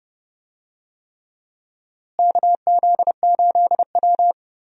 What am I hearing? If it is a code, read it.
Morse code, heard as a keyed tone, about 29 wpm, 700 Hz. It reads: KZ8W
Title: USA Call Signs - 4 Characters - Repeated 3x - 30wpm